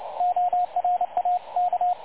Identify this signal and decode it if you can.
Morse code, heard as a keyed tone, about 27 wpm, 720 Hz, with radio noise. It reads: ORAK
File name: dlazaawansowanych / nagraniazpasma / websdr recording 3565.0kHz